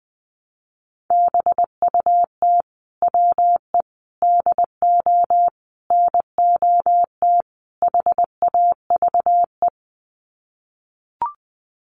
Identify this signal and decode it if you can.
Morse code, heard as a keyed tone, about 20 wpm, 700 Hz. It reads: BUT WE DO NOT HAVE
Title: Sets of 5 Words Encore2 20wpm